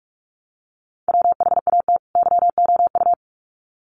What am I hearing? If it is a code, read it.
Morse code, heard as a keyed tone, about 44 wpm, 700 Hz. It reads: WHAT YOU
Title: Sets of 2 Words 45wpm